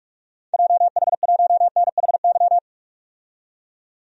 Morse code, heard as a keyed tone, about 46 wpm, 700 Hz. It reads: JH1NHY
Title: DXCC-100 Call Signs - 6 Characters - 45wpm